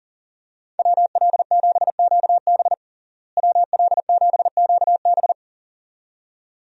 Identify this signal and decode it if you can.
Morse code, heard as a keyed tone, about 40 wpm, 700 Hz. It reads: WL7QB WL7QB
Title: USA Call Signs - 5 Characters - Repeated 2x - 40wpm